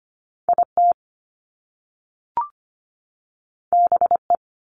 Morse code, heard as a keyed tone, about 25 wpm, 700 Hz. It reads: IT BE
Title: Top Two Letter Combinations in Words 25wpm